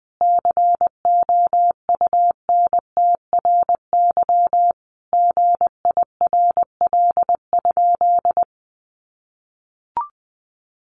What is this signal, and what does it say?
Morse code, heard as a keyed tone, about 20 wpm, 700 Hz. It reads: COUNTRY GIRL?
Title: Sentences from Top 1500 Words 20wpm